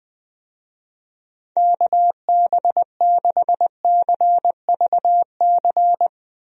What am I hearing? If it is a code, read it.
Morse code, heard as a keyed tone, about 20 wpm, 700 Hz. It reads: KB6CVC
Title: USA Call Signs - 6 Characters - Repeated 2x - 20wpm